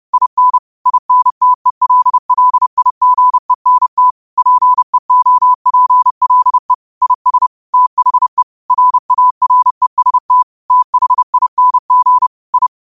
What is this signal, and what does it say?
Morse code, heard as a keyed tone, about 31 wpm, 1000 Hz. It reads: IN INTELLIGENT PEOPLE IS THE RAREST THING I
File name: qotdfiles / 30wpm